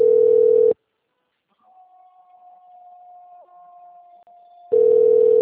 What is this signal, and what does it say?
Ringback tone (North America), 440 + 480 Hz.